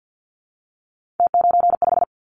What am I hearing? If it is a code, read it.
Morse code, heard as a keyed tone, about 50 wpm, 700 Hz. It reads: T95
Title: Callsign Prefixes for Top 100 DXCC - 50wpm